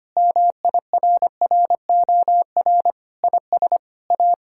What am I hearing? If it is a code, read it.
Morse code, heard as a keyed tone, about 25 wpm, 700 Hz. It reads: MIRROR IS A
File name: practice / 25wpm / Sentences with 4-Word Phrases - 25wpm